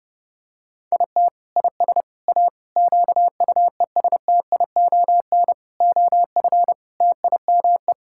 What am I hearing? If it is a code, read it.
Morse code, heard as a keyed tone, about 30 wpm, 700 Hz. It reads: IT IS A QUESTION OF TIME
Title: Sentences from Top 1000 Words - Review Part 6 - 30wpm